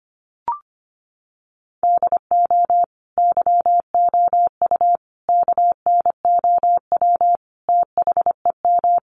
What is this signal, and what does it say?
Morse code, heard as a keyed tone, about 25 wpm, 700 Hz. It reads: DO YOU KNOW THEM
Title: Sentences from Top 1400 Words 25wpm